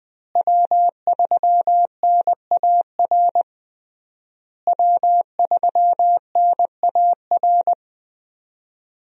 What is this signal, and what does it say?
Morse code, heard as a keyed tone, about 20 wpm, 700 Hz. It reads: W3NAR W3NAR
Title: USA Call Signs - 5 Characters - Repeated 3x - 20wpm